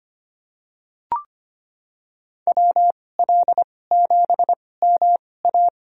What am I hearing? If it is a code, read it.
Morse code, heard as a keyed tone, about 25 wpm, spaced as for 18 wpm, 700 Hz. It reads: WL7MA
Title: USA Call Signs - 5 Characters - 2x Character Spacing - 25wpm